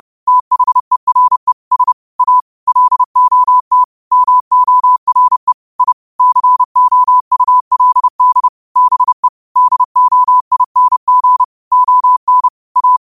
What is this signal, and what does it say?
Morse code, heard as a keyed tone, about 30 wpm, 1000 Hz. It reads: THERE S A LOT MORE I COULD BE DOING ON A
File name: qotdfiles / 30wpm